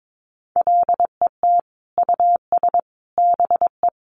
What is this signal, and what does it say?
Morse code, heard as a keyed tone, about 22 wpm, 700 Hz. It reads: LET US BE
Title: Sentences from Top 1400 Words 22wpm